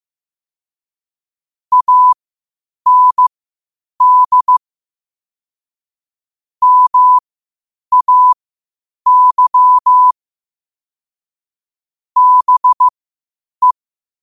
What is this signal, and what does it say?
Morse code, heard as a keyed tone, about 15 wpm, spaced as for 8 wpm, 1000 Hz. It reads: AND MAY BE
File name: qotdfiles / 8wpm